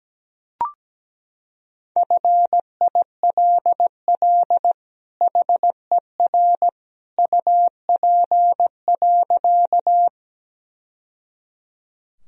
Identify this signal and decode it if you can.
Morse code, heard as a keyed tone, about 17 wpm, 700 Hz. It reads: FILL HER UP.
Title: Sentences from Top 1000 Words - Review Part 3 - 17wpm